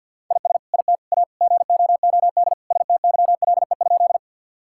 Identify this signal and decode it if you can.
Morse code, heard as a keyed tone, about 50 wpm, 700 Hz. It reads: IS IT A GOOD STYLE?
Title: Sentences from Top 1000 Words - Review Part 5 - 50wpm